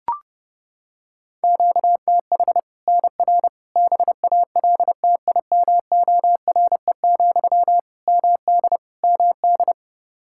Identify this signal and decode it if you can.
Morse code, heard as a keyed tone, about 30 wpm, 700 Hz. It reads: QTH NR BALTIMORE, MD MD
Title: Practice QSO - Exchange 1 - Beginning - 30wpm